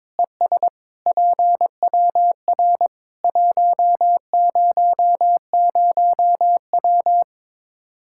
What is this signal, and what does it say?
Morse code, heard as a keyed tone, about 22 wpm, 700 Hz. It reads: ES PWR 100W
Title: Practice QSO - Exchange 2 - Info - 22wpm